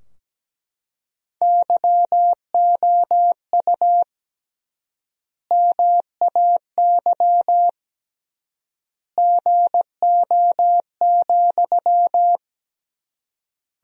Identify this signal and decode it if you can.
Morse code, heard as a keyed tone, about 17 wpm, 700 Hz. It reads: YOU MAY GO,
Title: 3x Word-Spacing - Sentences from Top 400 Words - 17wpm